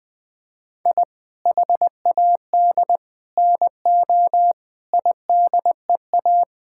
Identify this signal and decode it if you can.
Morse code, heard as a keyed tone, about 20 wpm, 700 Hz. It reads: I HAD NO IDEA
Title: Sentences with 8-Word Phrases - 20wpm